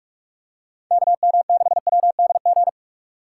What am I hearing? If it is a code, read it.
Morse code, heard as a keyed tone, about 47 wpm, 700 Hz. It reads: KM6WDG
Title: USA Call Signs - 6 Characters - 6x Character Spacing - 45wpm